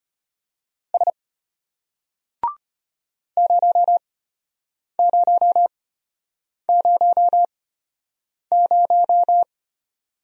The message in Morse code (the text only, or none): S 0000